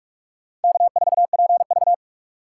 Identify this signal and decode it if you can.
Morse code, heard as a keyed tone, about 41 wpm, 700 Hz. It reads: K4PV